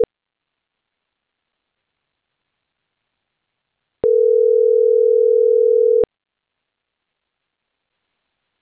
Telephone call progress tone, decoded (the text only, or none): ringback tone